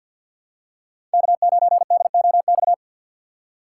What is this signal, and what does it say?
Morse code, heard as a keyed tone, about 51 wpm, 690 Hz. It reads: K9DOX
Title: USA Call Signs - 5 Characters - 3x Character Spacing - 50wpm